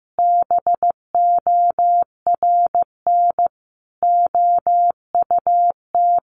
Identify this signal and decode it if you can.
Morse code, heard as a keyed tone, about 14 wpm, 700 Hz. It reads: BORN OUT
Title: Sentences from Top 1200 Words 15wpm